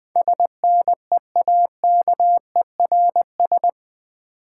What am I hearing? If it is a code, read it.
Morse code, heard as a keyed tone, about 20 wpm, 700 Hz. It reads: SNEAKERS
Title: Easy Words with Hints - 20wpm